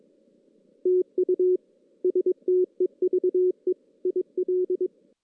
Morse code, heard as a keyed tone, about 22 wpm, 370 Hz. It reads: TU STEVE IL
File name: 2016 JAN CW NAQP / 2016 Jan naqp k5ot - 0-5